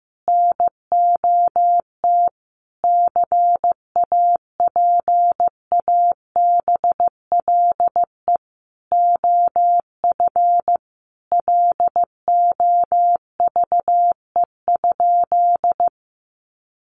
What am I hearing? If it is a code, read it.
Morse code, heard as a keyed tone, about 15 wpm, 700 Hz. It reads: NOT CAPABLE OF LOVE?
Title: Sentences from Top 1200 Words 15wpm